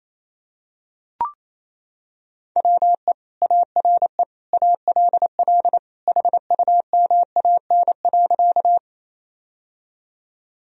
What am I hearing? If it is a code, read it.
Morse code, heard as a keyed tone, about 28 wpm, 700 Hz. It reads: WE ARE ALL HUMAN.
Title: Sentences from Top 1000 Words - Review Part 3 - 28wpm